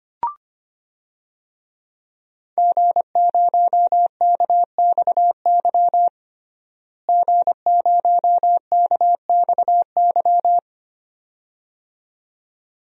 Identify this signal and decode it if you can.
Morse code, heard as a keyed tone, about 25 wpm, 700 Hz. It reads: G0KXY G0KXY
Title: DXCC-100 Call Signs - 5 Characters - Repeated 2x - 25wpm